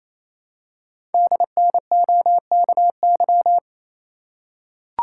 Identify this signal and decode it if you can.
Morse code, heard as a keyed tone, about 27 wpm, 700 Hz. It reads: DNOKY
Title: Code Groups - 5 Letters - 28wpm